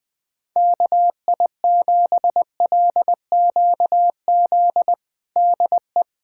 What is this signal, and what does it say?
Morse code, heard as a keyed tone, about 21 wpm, 700 Hz. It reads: KI7LQZ DE
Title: USA Callsign DE Callsign 20wpm